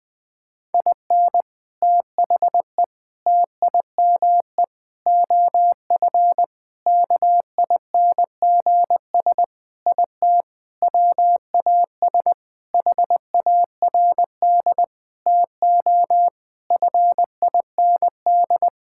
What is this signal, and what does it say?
Morse code, heard as a keyed tone, about 20 wpm, 700 Hz. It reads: IN THE TIME OF KINGS IT WAS HARD TO FIND
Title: Sentences with 3-Word Phrases - Spoken Beginning - 20wpm